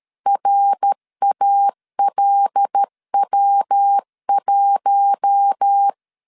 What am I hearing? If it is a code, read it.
Morse code, heard as a keyed tone, about 13 wpm, 800 Hz. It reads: RALW1
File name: CW / DET / 1 6 1500C S B 12wpm 3431